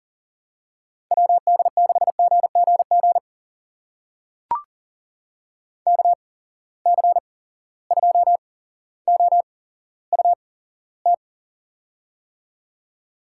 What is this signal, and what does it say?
Morse code, heard as a keyed tone, about 40 wpm, spaced as for 17 wpm, 700 Hz. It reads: WD6GGG KC2OUT
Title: USA Call Signs - 6 Characters - 8x Character Spacing - 40wpm